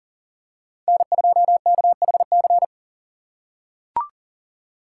Morse code, heard as a keyed tone, about 40 wpm, 700 Hz. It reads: N2KHC
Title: USA Call Signs - 5 Characters - 4x Character Spacing - 40wpm